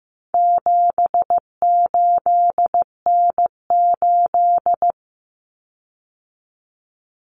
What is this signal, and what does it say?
Morse code, heard as a keyed tone, about 15 wpm, 700 Hz. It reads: 78N8
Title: Common Cut Numbers - 4-Digit 15wpm